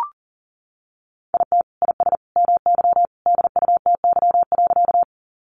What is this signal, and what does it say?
Morse code, heard as a keyed tone, about 40 wpm, 700 Hz. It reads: IT IS MY DUTY.
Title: Sentences from Top 1500 Words 40wpm